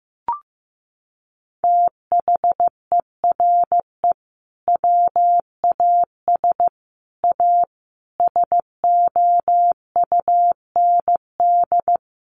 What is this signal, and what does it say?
Morse code, heard as a keyed tone, about 15 wpm, 700 Hz. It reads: THERE WAS A SOUND
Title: A Sound Of Thunder 15wpm